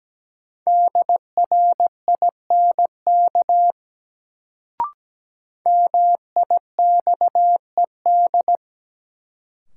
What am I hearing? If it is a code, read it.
Morse code, heard as a keyed tone, about 17 wpm, 700 Hz. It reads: DRINK MIXED
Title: Sentences from Top 1500 Words - New Word Drill - 17wpm